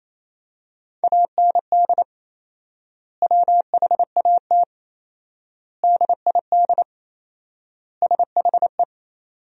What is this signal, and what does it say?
Morse code, heard as a keyed tone, about 28 wpm, 700 Hz. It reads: AND WHAT DID SHE
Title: 4x Word-Spacing - Sentences from Top 200 Words - 28wpm